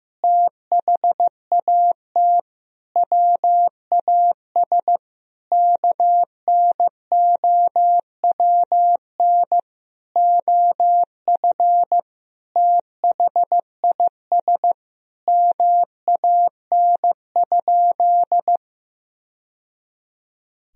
Morse code, heard as a keyed tone, about 15 wpm, 700 Hz. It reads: THAT WAS KNOWN OF THIS MAN?
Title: Sentences from Top 1000 Words - Review Part 3 - 15wpm